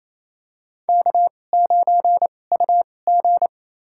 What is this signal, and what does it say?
Morse code, heard as a keyed tone, about 28 wpm, spaced as for 20 wpm, 700 Hz. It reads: K9UG
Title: USA Call Signs - 4 Characters - 2x Character Spacing - 28wpm